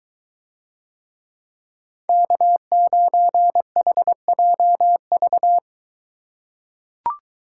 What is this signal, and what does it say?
Morse code, heard as a keyed tone, about 21 wpm, 700 Hz. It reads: K9HJV